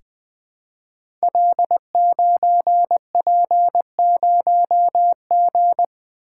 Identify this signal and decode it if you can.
Morse code, heard as a keyed tone, about 20 wpm, 700 Hz. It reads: L9P0G
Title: Code Groups - 5 Letter:Numbers - 20wpm